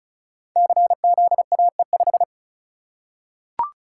Morse code, heard as a keyed tone, about 36 wpm, 700 Hz. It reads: CZAE5
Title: Code Groups - 5 Letter:Numbers - 35wpm